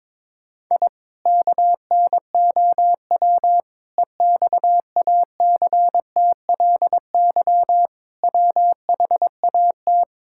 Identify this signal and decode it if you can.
Morse code, heard as a keyed tone, about 21 wpm, 700 Hz. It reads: I KNOW EXACTLY WHAT
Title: Sentences from Top 1000 Words - Review Part 6 - 22wpm